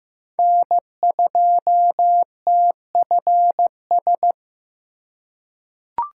Morse code, heard as a keyed tone, about 15 wpm, 700 Hz. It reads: N2TFS
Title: USA Call Signs - 5 Characters - 6x Character Spacing - 15wpm